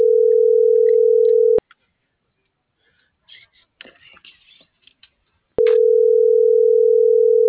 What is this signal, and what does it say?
Ringback tone (North America), 440 + 480 Hz.
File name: CENTRAL NAVIERAS / out oscar.monta.o